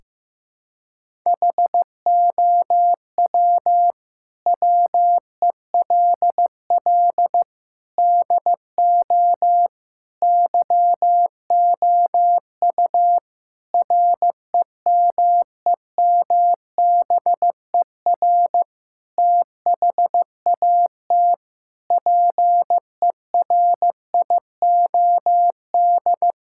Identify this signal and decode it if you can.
Morse code, heard as a keyed tone, about 15 wpm, 700 Hz. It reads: HOW WELL DO YOU REMEMBER THAT PERIOD
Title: Sentences from Top 1000 Words - Review Part 5 - 15wpm